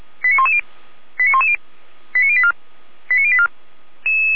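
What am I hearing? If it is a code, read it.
Five-tone selective call (ZVEI), heard as quick runs of five tones, tones of about 70 ms. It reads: 89110 89110 89084 89084
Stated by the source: ZVEI five-tone call 89097 received on SDR.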